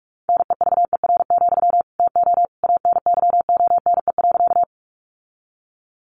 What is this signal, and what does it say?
Morse code, heard as a keyed tone, about 44 wpm, 700 Hz. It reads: NEVER, TO ANYONE.
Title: Sentences from Top 900 Words 45wpm